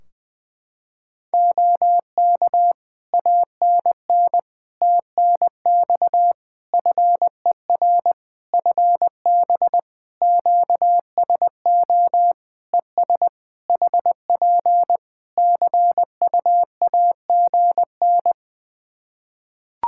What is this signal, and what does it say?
Morse code, heard as a keyed tone, about 20 wpm, 700 Hz. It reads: OK ANN TNX FER FB QSO ES HP CUAGN
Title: K1USN SST QSO Practice 20wpm